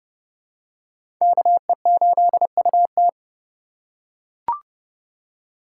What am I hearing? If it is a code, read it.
Morse code, heard as a keyed tone, about 30 wpm, 700 Hz. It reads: KE8UT
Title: USA Call Signs - 2x Character Spacing - 30wpm